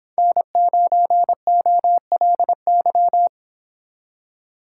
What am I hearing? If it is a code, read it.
Morse code, heard as a keyed tone, about 26 wpm, 700 Hz. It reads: N9OLY